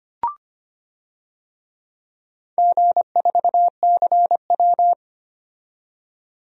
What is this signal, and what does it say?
Morse code, heard as a keyed tone, about 25 wpm, 700 Hz. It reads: G4CW
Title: DXCC-100 Call Signs - 4 Characters - 25wpm